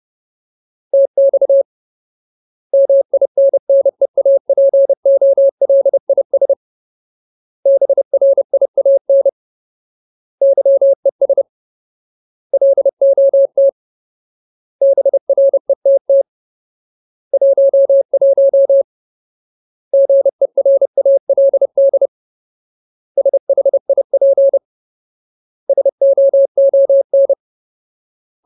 Morse code, heard as a keyed tone, about 30 wpm, 550 Hz. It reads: TX MINNEAPOLIS BRIAN YES LOT BRETT 11 GERALD SHIP SOON
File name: Ultimate Word List / 30WPM Ultimate